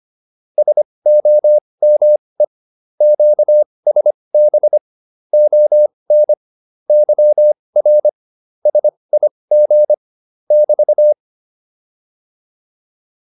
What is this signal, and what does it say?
Morse code, heard as a keyed tone, about 25 wpm, spaced as for 20 wpm, 600 Hz. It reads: SOME QSB ON YR SIG <BT>
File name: QSO407 20